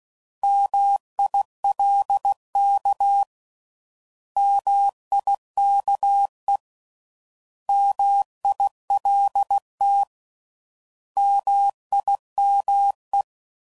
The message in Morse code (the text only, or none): MILK MIKE MILT MIME